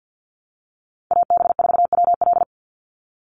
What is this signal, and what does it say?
Morse code, heard as a keyed tone, about 48 wpm, 700 Hz. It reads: AB4WL
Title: NA SOTA Chaser Callsigns 50wpm